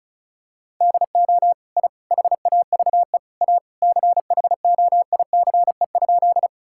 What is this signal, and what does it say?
Morse code, heard as a keyed tone, about 36 wpm, 700 Hz. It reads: DO I HAVE A CHOICE?
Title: Sentences from Top 1000 Words - Review Part 6 - 35wpm